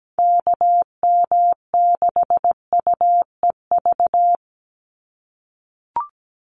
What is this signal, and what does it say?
Morse code, heard as a keyed tone, about 17 wpm, 700 Hz. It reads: KM6UEV
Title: USA Call Signs 17wpm